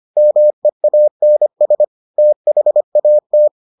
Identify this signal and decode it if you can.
Morse code, heard as a keyed tone, about 25 wpm, 600 Hz. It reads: MEANS THAT